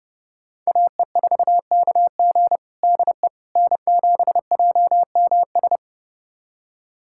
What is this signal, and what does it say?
Morse code, heard as a keyed tone, about 30 wpm, 700 Hz. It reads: AE4KG DE N7JMS
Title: USA Callsign DE Callsign - 5 Characters - 30wpm